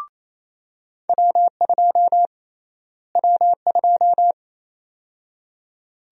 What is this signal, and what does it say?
Morse code, heard as a keyed tone, about 28 wpm, 700 Hz. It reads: W2 W2
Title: USA Call Sign Prefixes - Repeated 2x - 28wpm